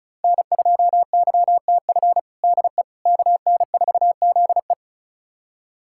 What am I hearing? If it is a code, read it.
Morse code, heard as a keyed tone, about 34 wpm, 700 Hz. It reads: N2YTF DE KN4ZE
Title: USA Callsign DE Callsign - 5 Characters - 35wpm